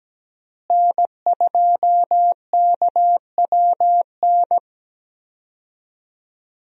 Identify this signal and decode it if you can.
Morse code, heard as a keyed tone, about 17 wpm, 700 Hz. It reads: N2KWN
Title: USA Call Signs - 5 Characters - Repeated 3x - 17wpm